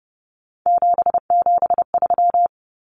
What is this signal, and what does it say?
Morse code, heard as a keyed tone, about 30 wpm, 700 Hz. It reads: 773
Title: Common Cut Numbers - 3-Digit 30wpm